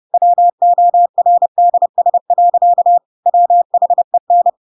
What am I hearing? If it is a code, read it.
Morse code, heard as a keyed tone, about 30 wpm, 700 Hz. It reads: WORDS. WHEN
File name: Pinocchio / Pinocchio ch8 30 30 ik1hge